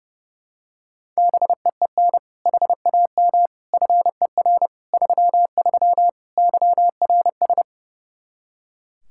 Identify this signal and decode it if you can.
Morse code, heard as a keyed tone, about 32 wpm, 700 Hz. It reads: BEEN HAM FER 33 YRS
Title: Practice QSO - Exchange 3 - More Info - 30wpm